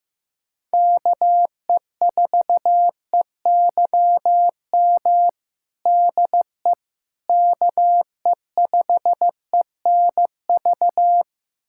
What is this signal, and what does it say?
Morse code, heard as a keyed tone, about 16 wpm, 700 Hz. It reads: KE4EYM DE KE5ENV
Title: USA Callsign DE Callsign - 6 Characters - 15wpm